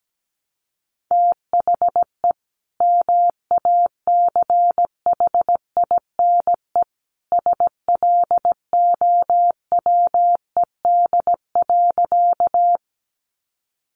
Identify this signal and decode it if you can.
Morse code, heard as a keyed tone, about 17 wpm, 700 Hz. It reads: THE MACHINE SLOWED.
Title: A Sound Of Thunder 17wpm